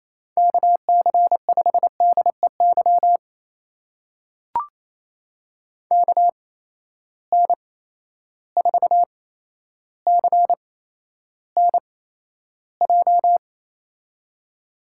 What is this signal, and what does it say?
Morse code, heard as a keyed tone, about 28 wpm, spaced as for 12 wpm, 700 Hz. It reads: KC5DEY KN4CNJ
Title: USA Call Signs - 8x Character Spacing - 28wpm